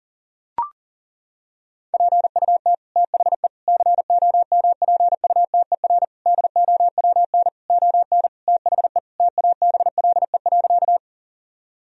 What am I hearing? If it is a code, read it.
Morse code, heard as a keyed tone, about 40 wpm, 700 Hz. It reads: PUT THE COMPUTER DOWN ON THE TABLE.
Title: Sentences from Top 1000 Words - Review Part 3 - 40wpm